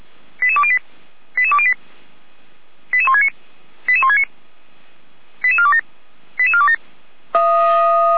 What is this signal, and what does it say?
Five-tone selective call (ZVEI), heard as quick runs of five tones, tones of about 70 ms. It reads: 88208 88208 88169 88169 88427 88427
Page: /455/SDR/ZVEI/